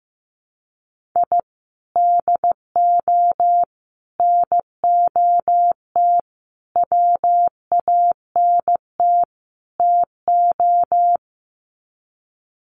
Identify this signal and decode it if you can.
Morse code, heard as a keyed tone, about 15 wpm, 700 Hz. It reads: I DO NOT WANT TO
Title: Sets of 5 Words Encore2 15wpm